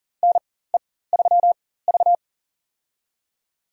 Morse code, heard as a keyed tone, about 39 wpm, spaced as for 18 wpm, 700 Hz. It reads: NE3V
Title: USA Call Signs - 4 Characters - 4x Character Spacing - 40wpm